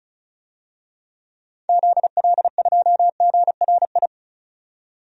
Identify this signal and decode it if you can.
Morse code, heard as a keyed tone, about 36 wpm, 700 Hz. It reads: ZL2GRI
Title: DXCC-100 Call Signs - 6 Characters - 35wpm